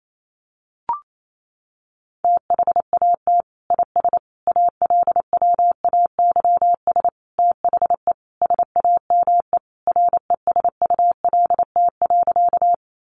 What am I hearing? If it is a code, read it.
Morse code, heard as a keyed tone, about 28 wpm, 700 Hz. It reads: THAT IS ALWAYS THE SAME RESULT.
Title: Sentences from Top 700 Words 28wpm